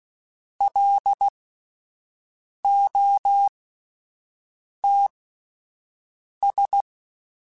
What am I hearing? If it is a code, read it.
Morse code, heard as a keyed tone, about 16 wpm, spaced as for 5 wpm, 800 Hz. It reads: LOTS
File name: Spaced Characters / Spaced Characters-mp3s / Weather5 12 sp @10